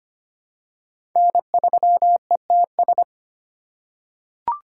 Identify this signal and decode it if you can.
Morse code, heard as a keyed tone, about 25 wpm, 700 Hz. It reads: N3ETS
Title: USA Call Signs - 5 Characters - 2x Character Spacing - 25wpm